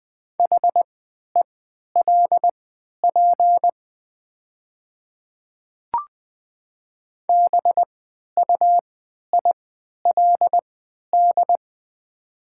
Morse code, heard as a keyed tone, about 20 wpm, spaced as for 11 wpm, 700 Hz. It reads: HELP BUILD
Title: 3x Character Spacing - Top 200 Words - 20wpm